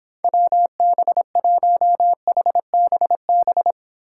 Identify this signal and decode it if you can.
Morse code, heard as a keyed tone, about 26 wpm, 700 Hz. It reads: WB1HBB